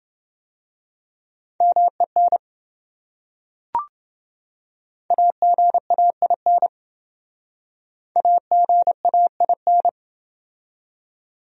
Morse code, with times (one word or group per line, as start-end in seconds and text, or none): 1.6-2.4 MEN
5.1-6.7 AGAIN
8.2-9.9 AGAIN